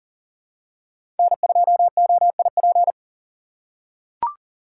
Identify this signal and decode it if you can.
Morse code, heard as a keyed tone, about 40 wpm, 700 Hz. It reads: N2OIP
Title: USA Call Signs - 5 Characters - 6x Character Spacing - 40wpm